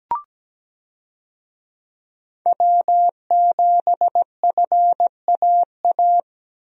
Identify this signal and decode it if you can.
Morse code, heard as a keyed tone, about 17 wpm, 700 Hz. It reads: W7FAA
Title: USA Call Signs - 5 Characters - 17wpm